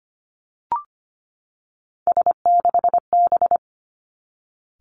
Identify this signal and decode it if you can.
Morse code, heard as a keyed tone, about 25 wpm, 700 Hz. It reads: S6B
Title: Three Challenging Letter-Number Combinations 25wpm